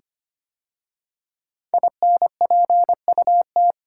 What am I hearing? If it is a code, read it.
Morse code, heard as a keyed tone, about 25 wpm, 700 Hz. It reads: INPUT
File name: 5 Letter Words - First 3 Letters Given - 25wpm